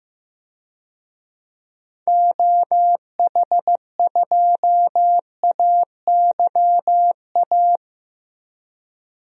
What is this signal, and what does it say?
Morse code, heard as a keyed tone, about 15 wpm, 700 Hz. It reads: OH2AYA
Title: DXCC-100 Call Signs - 6 Characters - 15wpm